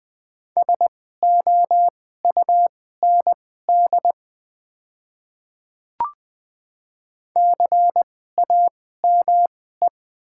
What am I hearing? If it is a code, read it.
Morse code, heard as a keyed tone, about 20 wpm, spaced as for 14 wpm, 700 Hz. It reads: SOUND CAME
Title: 2x Character Spacing - Top 300 Words - 20wpm